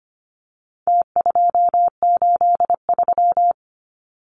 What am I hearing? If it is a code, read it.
Morse code, heard as a keyed tone, about 25 wpm, 700 Hz. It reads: T283
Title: Numbers - Contest Style - 4-Digit 25wpm